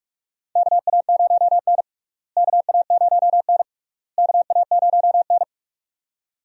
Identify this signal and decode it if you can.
Morse code, heard as a keyed tone, about 48 wpm, 700 Hz. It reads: KA0N KA0N KA0N
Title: USA Call Signs - 4 Characters - Repeated 3x - 45wpm